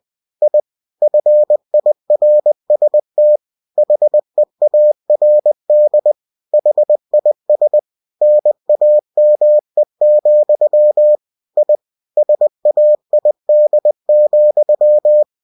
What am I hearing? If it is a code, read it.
Morse code, heard as a keyed tone, about 20 wpm, 600 Hz. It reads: I FIRST HEARD HIS NAME, I SAID,
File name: cw-books / winnie-the-pooh / winnie-20wpm-page-0000